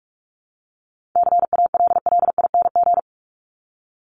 Morse code, heard as a keyed tone, about 44 wpm, 700 Hz. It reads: CALLING
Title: ING Suffix - Top 100 Words with Base Words 45wpm